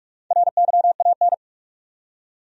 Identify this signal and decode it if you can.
Morse code, heard as a keyed tone, about 46 wpm, 700 Hz. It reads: RYAN
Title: USA Common Names - Max 4 Letters - 45wpm